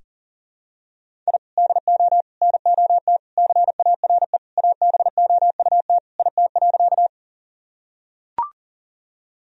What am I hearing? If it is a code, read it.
Morse code, heard as a keyed tone, about 40 wpm, 700 Hz. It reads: I DO NOT CARE ABOUT IT.
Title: Sentences from Top 1000 Words - Review Part 1 - 40wpm